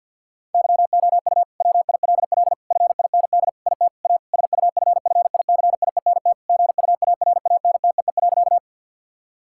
Morse code, heard as a keyed tone, about 50 wpm, 700 Hz. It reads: YOU WILL FIND IT A SUFFICIENT GUARANTEE.
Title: Sentences from Top 1600 Words 50wpm